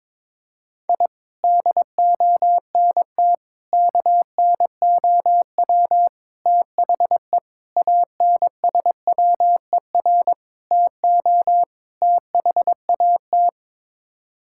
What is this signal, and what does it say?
Morse code, heard as a keyed tone, about 22 wpm, 700 Hz. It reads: I DONT KNOW THE ANSWER TO THAT
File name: practice / 22wpm / Sets of 7 Words - 22wpm